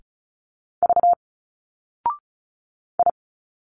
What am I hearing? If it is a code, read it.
Morse code, heard as a keyed tone, about 35 wpm, 700 Hz. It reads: V I
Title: Single Letters 35wpm